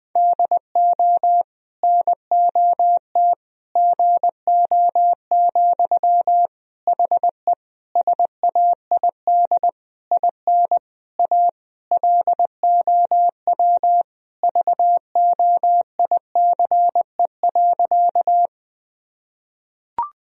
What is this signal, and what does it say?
Morse code, heard as a keyed tone, about 21 wpm, 700 Hz. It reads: DO NOT GO, HE SAID IN A LOW VOICE.
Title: Sentences from Top 1000 Words - Review Part 1 - 20wpm